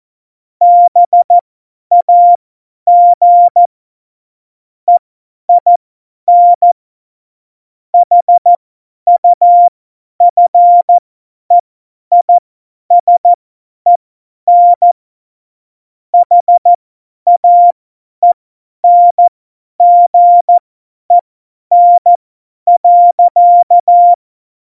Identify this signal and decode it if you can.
Morse code, heard as a keyed tone, about 14 wpm, spaced as for 10 wpm, 700 Hz. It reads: BAG EIN HUFEISEN HAENGEN.
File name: Morsekurs / Witze / Witz 26. Januar 2021, 50 F70